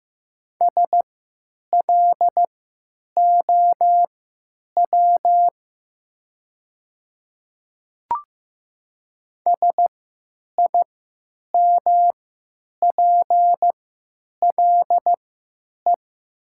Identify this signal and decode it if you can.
Morse code, heard as a keyed tone, about 15 wpm, spaced as for 9 wpm, 700 Hz. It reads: SLOW SIMPLE
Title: 3x Character Spacing - Top 500 Words - 15wpm